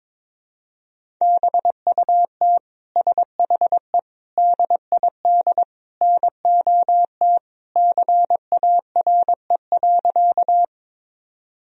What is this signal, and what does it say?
Morse code, heard as a keyed tone, about 22 wpm, 700 Hz. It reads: BUT SHE DID NOT CARE.
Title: Sentences from Top 1000 Words - Review Part 1 - 22wpm